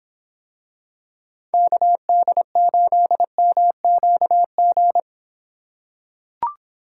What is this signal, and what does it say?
Morse code, heard as a keyed tone, about 26 wpm, 700 Hz. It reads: KD8MQG